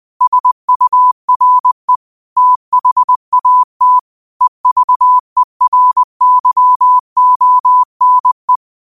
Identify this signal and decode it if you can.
Morse code, heard as a keyed tone, about 20 wpm, 1000 Hz. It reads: SURE THAT EVERYONE